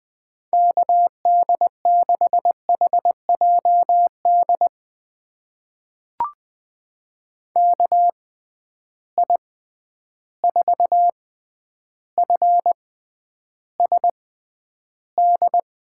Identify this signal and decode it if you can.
Morse code, heard as a keyed tone, about 20 wpm, spaced as for 10 wpm, 700 Hz. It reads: KD6HJD KI4FSD
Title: USA Call Signs - 6 Characters - 6x Character Spacing - 20wpm